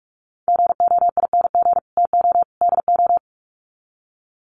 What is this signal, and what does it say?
Morse code, heard as a keyed tone, about 44 wpm, 700 Hz. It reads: GOING TO DO
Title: Sets of 3 Words 45wpm